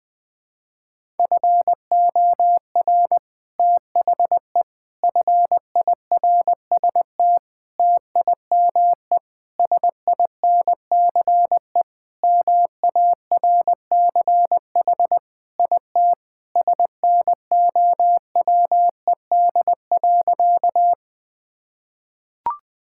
Morse code, heard as a keyed tone, about 20 wpm, 700 Hz. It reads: FOR THE FIRST TIME SINCE MARCH IT SNOWED.
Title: Sentences with 5-Word Phrases - 20wpm